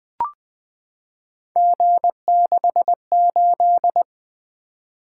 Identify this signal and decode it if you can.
Morse code, heard as a keyed tone, about 20 wpm, 700 Hz. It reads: G68
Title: Three Letter-Number Combinations - 20wpm